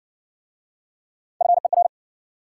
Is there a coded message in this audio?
Yes: Morse code from 1.4 to 1.9 s, heard as a keyed tone, about 58 wpm, 700 Hz. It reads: FER